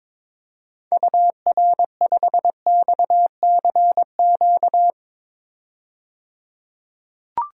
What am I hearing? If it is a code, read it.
Morse code, heard as a keyed tone, about 22 wpm, 700 Hz. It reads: UR5XCQ
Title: DXCC-100 Call Signs - 6 Characters - 22wpm